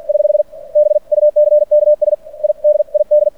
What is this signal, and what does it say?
Morse code, heard as a keyed tone, about 50 wpm, spaced as for 44 wpm, 590 Hz. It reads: <HH> BUCKS I DID
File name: archived / CRSnet